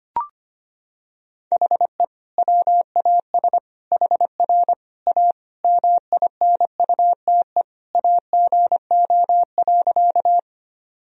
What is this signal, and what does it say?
Morse code, heard as a keyed tone, about 25 wpm, 700 Hz. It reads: HE WAS HR A MINUTE AGO.
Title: Sentences from Top 500 Words and Abbreviations 25wpm